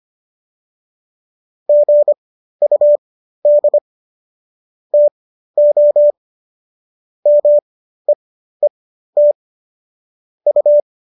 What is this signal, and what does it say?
Morse code, heard as a keyed tone, about 25 wpm, spaced as for 13 wpm, 590 Hz. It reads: GUD TO MEET U